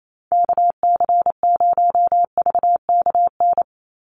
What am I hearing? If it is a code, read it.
Morse code, heard as a keyed tone, about 28 wpm, 700 Hz. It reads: KC0VKN
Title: Call Signs - 28wpm